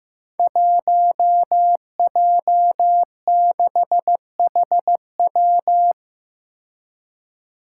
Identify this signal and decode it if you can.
Morse code, heard as a keyed tone, about 15 wpm, 700 Hz. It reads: 1J6HW
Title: Five Challenging Letter-Number Combinations 15wpm